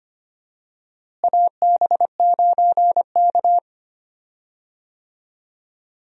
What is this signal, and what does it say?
Morse code, heard as a keyed tone, about 25 wpm, 700 Hz. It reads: AB9K
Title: USA Call Signs - 4 Characters - Repeated 2x - 25wpm